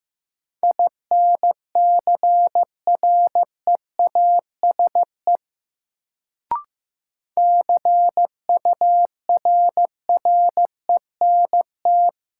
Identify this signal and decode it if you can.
Morse code, heard as a keyed tone, about 16 wpm, 700 Hz. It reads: INCREASE CURRENT
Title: Sentences from Top 1000 Words - Review Part 4 - New Word Drill - 15wpm